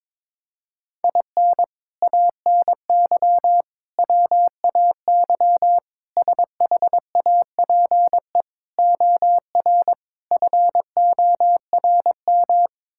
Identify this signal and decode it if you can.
Morse code, heard as a keyed tone, about 22 wpm, 700 Hz. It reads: IN ANY WAY SHAPE OR FORM
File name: practice / 22wpm / Sets of 6 Words - 22wpm